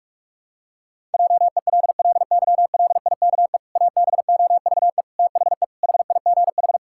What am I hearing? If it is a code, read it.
Morse code, heard as a keyed tone, about 46 wpm, 700 Hz. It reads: JELLYLIKE ABOVE THE HIGH
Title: Pangram Challenge v2 - 45wpm